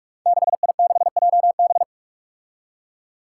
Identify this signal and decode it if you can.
Morse code, heard as a keyed tone, about 46 wpm, 700 Hz. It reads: 6I6JB
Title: Five Challenging Letter-Number Combinations 45wpm